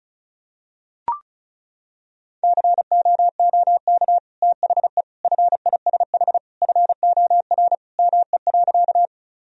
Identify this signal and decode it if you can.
Morse code, heard as a keyed tone, about 34 wpm, 700 Hz. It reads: COOK THE FISH FOR ME.
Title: Sentences from Top 1000 Words - Review Part 4 - 35wpm